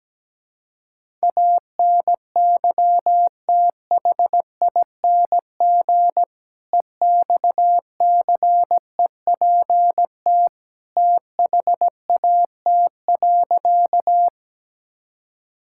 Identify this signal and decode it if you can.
Morse code, heard as a keyed tone, about 17 wpm, 700 Hz. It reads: ANYTHING EXCEPT THAT.
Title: Sentences from Top 1600 Words 17wpm